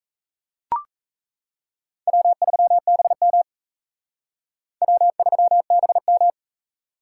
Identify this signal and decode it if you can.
Morse code, heard as a keyed tone, about 34 wpm, 700 Hz. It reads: W3BM W3BM
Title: Speed Racing - USA 1x2 Call Signs 28wpm